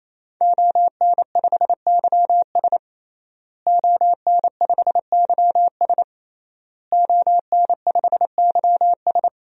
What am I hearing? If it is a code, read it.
Morse code, heard as a keyed tone, about 28 wpm, 700 Hz. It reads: ON5YS ON5YS ON5YS
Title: DXCC-100 Call Signs - 5 Characters - Repeated 3x - 28wpm